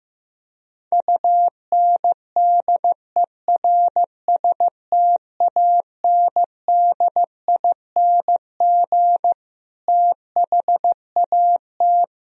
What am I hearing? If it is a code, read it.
Morse code, heard as a keyed tone, about 15 wpm, 700 Hz. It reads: UNDERSTANDING THAT
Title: Sentences from Top 1000 Words - Review Part 3 - 15wpm